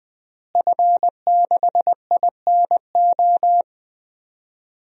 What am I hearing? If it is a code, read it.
Morse code, heard as a keyed tone, about 20 wpm, 700 Hz. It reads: F6INO
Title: DXCC-100 Call Signs - 5 Characters - Repeated 2x - 20wpm